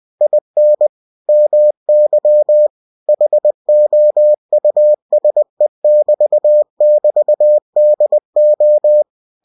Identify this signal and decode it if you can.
Morse code, heard as a keyed tone, about 20 wpm, 600 Hz. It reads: IN MY HOUSE <BT> <BT> DO
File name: cw-books / winnie-the-pooh / winnie-20wpm-page-0019